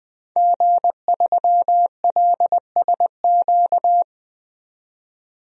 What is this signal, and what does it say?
Morse code, heard as a keyed tone, about 20 wpm, 700 Hz. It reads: G3LSQ
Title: DXCC-100 Call Signs - 5 Characters - 20wpm